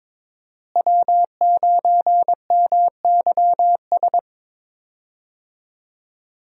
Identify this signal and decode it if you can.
Morse code, heard as a keyed tone, about 22 wpm, 700 Hz. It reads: W9MYS
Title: USA Call Signs - 5 Characters - Repeated 2x - 22wpm